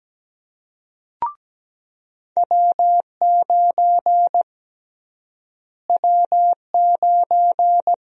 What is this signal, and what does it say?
Morse code, heard as a keyed tone, about 17 wpm, 700 Hz. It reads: W9 W9
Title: USA Call Sign Prefixes - Repeated 2x - 17wpm